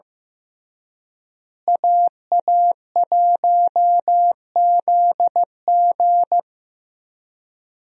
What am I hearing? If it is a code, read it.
Morse code, heard as a keyed tone, about 15 wpm, 700 Hz. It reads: AA1ZG
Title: USA Call Signs - 5 Characters - Repeated 3x - 15wpm